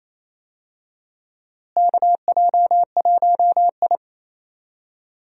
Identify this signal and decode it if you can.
Morse code, heard as a keyed tone, about 28 wpm, 700 Hz. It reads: KJ1I
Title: USA Call Signs - 4 Characters - Repeated 3x - 28wpm